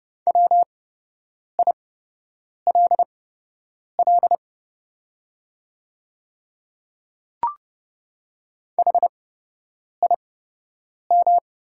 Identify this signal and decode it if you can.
Morse code, heard as a keyed tone, about 30 wpm, spaced as for 8 wpm, 690 Hz. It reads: WILL HIM
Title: 8x Character Spacing - Top 100 Words - 30wpm